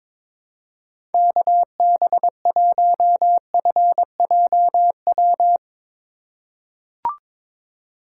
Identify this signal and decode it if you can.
Morse code, heard as a keyed tone, about 22 wpm, 700 Hz. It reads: KB1FJW
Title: USA Call Signs - 6 Characters - 2x Character Spacing - 22wpm